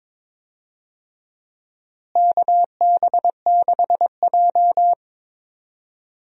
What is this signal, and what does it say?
Morse code, heard as a keyed tone, about 22 wpm, 700 Hz. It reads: KB6J
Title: USA Call Signs - 4 Characters - Repeated 2x - 22wpm